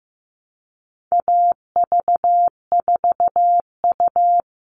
Morse code, heard as a keyed tone, about 15 wpm, 700 Hz. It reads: AV4U
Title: All Cut Numbers - 4-Digit 15wpm